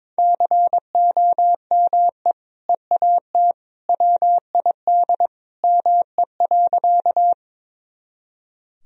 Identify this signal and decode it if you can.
Morse code, heard as a keyed tone, about 22 wpm, 700 Hz. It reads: COME EAT WID ME.
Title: Sentences from Top 300 Words and Abbreviations 22wpm